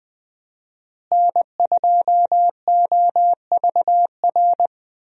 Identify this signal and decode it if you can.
Morse code, heard as a keyed tone, about 20 wpm, 700 Hz. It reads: N2OVR
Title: USA Call Signs - 5 Characters - Repeated 3x - 20wpm